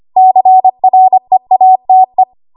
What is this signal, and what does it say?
Morse code, heard as a keyed tone, about 25 wpm, 750 Hz. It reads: CREATE
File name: Archive / 25 WPM / 25WPM